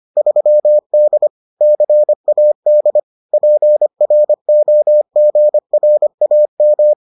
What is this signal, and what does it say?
Morse code, heard as a keyed tone, about 25 wpm, 600 Hz. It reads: 3D CAD PROGRAM